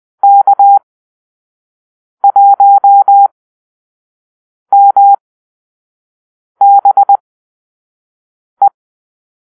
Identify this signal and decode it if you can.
Morse code, heard as a keyed tone, about 20 wpm, spaced as for 5 wpm, 800 Hz. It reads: K1MBE
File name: arrl / Call-Sign Drill-4